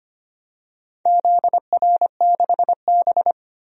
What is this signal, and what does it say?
Morse code, heard as a keyed tone, about 25 wpm, 700 Hz. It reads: ZR6B
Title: DXCC-100 Call Signs - 4 Characters - 25wpm